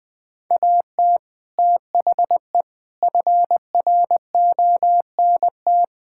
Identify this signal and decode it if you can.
Morse code, heard as a keyed tone, about 20 wpm, 700 Hz. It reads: AT THE FRONT
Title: Build A Phrase - 5-Word Phrases - 20wpm